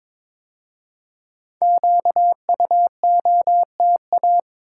Morse code, heard as a keyed tone, about 22 wpm, 700 Hz. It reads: QUOTA
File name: 5 Letter Words - First 3 Letters Given - 22wpm